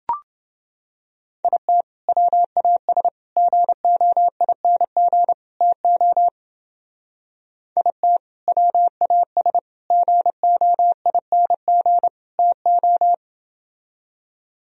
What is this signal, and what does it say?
Morse code, heard as a keyed tone, about 29 wpm, 700 Hz. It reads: IT WAS GOING TO IT WAS GOING TO